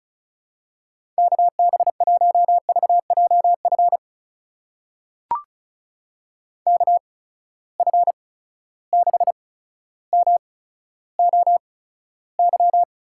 Morse code, heard as a keyed tone, about 36 wpm, spaced as for 15 wpm, 700 Hz. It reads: KB1VJF KF6MOY
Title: USA Call Signs - 6 Characters - 8x Character Spacing - 35wpm